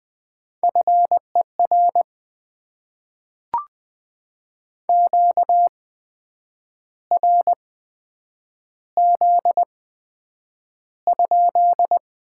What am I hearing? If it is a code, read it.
Morse code, heard as a keyed tone, about 20 wpm, spaced as for 8 wpm, 700 Hz. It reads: FER QRZ?
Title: Top 90 QSO Elements - 8x Character Spacing - 20wpm